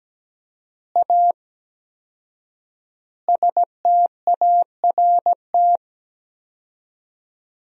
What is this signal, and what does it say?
Morse code, heard as a keyed tone, about 17 wpm, 700 Hz. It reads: A START
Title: 4x Word-Spacing - Sentences from Top 300 Words - 17wpm